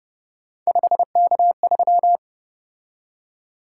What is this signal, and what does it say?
Morse code, heard as a keyed tone, about 30 wpm, 700 Hz. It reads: 5K3
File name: Speed-Racing - Three Letter-Number Combinations - 25wpm